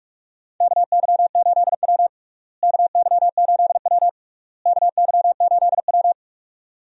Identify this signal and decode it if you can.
Morse code, heard as a keyed tone, about 46 wpm, 700 Hz. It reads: KY8W KY8W KY8W
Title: USA Call Signs - 4 Characters - Repeated 3x - 45wpm